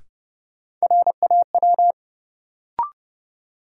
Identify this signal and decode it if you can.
Morse code, heard as a keyed tone, about 30 wpm, 700 Hz. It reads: RAW
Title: Sentences from Top 1500 Words - New Word Drill - 30wpm